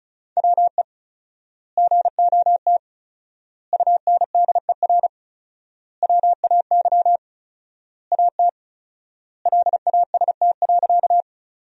Morse code, heard as a keyed tone, about 36 wpm, 700 Hz. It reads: WE GOT UNDER WAY AT LAST.
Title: 4x Word-Spacing - Sentences from Top 300 Words - 35wpm